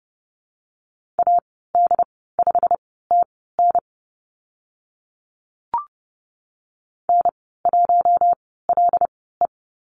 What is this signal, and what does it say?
Morse code, heard as a keyed tone, about 30 wpm, spaced as for 17 wpm, 700 Hz. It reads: AD5TN N1LE
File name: Lesson 15g - Number 5 - Call Signs 30wpm